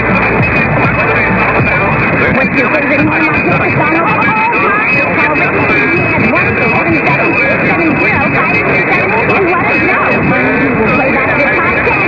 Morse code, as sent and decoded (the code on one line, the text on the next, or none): none